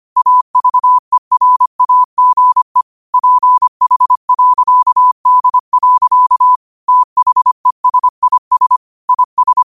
Morse code, heard as a keyed tone, about 25 wpm, 1000 Hz. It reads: AVERAGE PH.D. THESIS IS
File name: qotdfiles / 25wpm